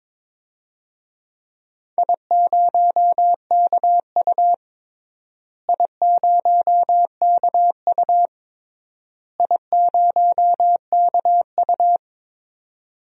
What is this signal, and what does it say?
Morse code, heard as a keyed tone, about 22 wpm, 700 Hz. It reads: I0KU I0KU I0KU
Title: DXCC-100 Call Signs - 4 Characters - Repeated 3x - 22wpm